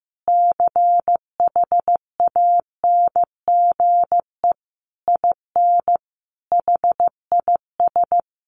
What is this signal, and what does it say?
Morse code, heard as a keyed tone, about 15 wpm, 700 Hz. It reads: CHANGE IN HIS
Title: Sentences from Top 1100 Words 15wpm